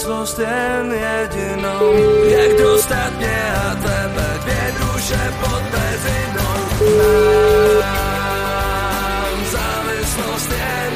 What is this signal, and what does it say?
Ringback tone (Europe, 425 Hz), with noise.